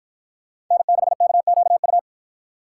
Morse code, heard as a keyed tone, about 53 wpm, 700 Hz. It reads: N6KQU